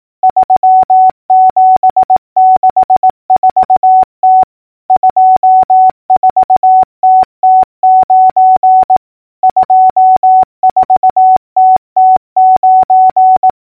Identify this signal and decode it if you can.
Morse code, heard as a keyed tone, about 18 wpm, 750 Hz. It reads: 3764T 24TT9 24TT9
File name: numbers / V2-morse-clicks